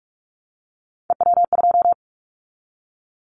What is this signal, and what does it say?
Morse code, heard as a keyed tone, about 44 wpm, 700 Hz. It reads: EW2
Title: Callsign Prefixes for Top 100 DXCC - 45wpm